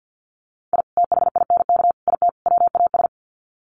Morse code, heard as a keyed tone, about 50 wpm, 700 Hz. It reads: I THINK IT WAS
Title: Sets of 4 Words 50wpm